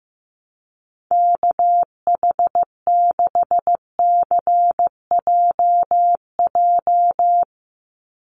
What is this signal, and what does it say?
Morse code, heard as a keyed tone, about 15 wpm, 700 Hz. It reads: KH6CJJ
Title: Call Signs Repeated 2x - 15wpm